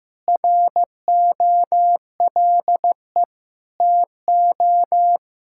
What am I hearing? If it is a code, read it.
Morse code, heard as a keyed tone, about 15 wpm, 700 Hz. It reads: ROLE TO
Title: Sentences from Top 1000 Words - Review Part 6 - 15wpm